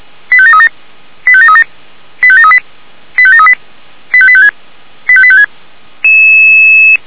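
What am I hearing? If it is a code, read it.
Five-tone selective call (ZVEI), heard as quick runs of five tones, tones of about 70 ms. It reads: 85737 85737 85738 85738 85856 85856
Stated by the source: ZVEI five-tone selective call 85854 received with an SDR, archived by Feuerwehr Orlamünde (fire brigade).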